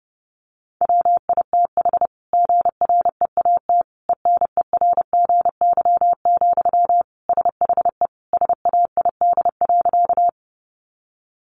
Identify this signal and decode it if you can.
Morse code, heard as a keyed tone, about 30 wpm, 700 Hz. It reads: WITH GREAT ENERGY, SHE SAID.
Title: Sentences from Top 900 Words 30wpm